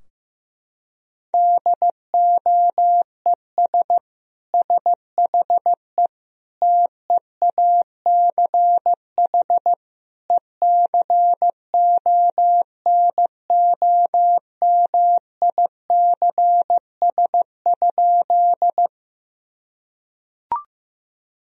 Morse code, heard as a keyed tone, about 14 wpm, 700 Hz. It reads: DOES SHE TEACH ECONOMICS?
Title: Sentences from Top 1000 Words - Review Part 3 - 15wpm